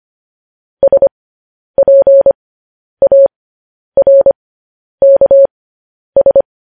Morse code, heard as a keyed tone, about 25 wpm, spaced as for 10 wpm, 550 Hz. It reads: SPARKS
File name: podcast / News headlines-1x-25x10